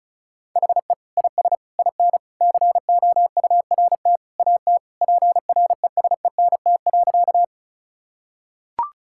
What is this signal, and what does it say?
Morse code, heard as a keyed tone, about 36 wpm, 700 Hz. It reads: HE IS IN COURT AT PRESENT.
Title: Sentences from Top 1600 Words 35wpm